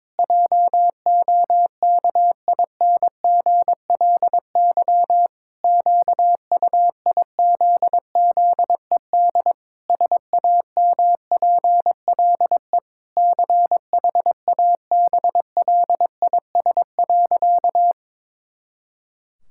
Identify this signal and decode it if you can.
Morse code, heard as a keyed tone, about 22 wpm, 700 Hz. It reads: JOKINGLY QUIZZED SAMPLE CHABLIS.
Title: Pangram Challenge v2 - 22wpm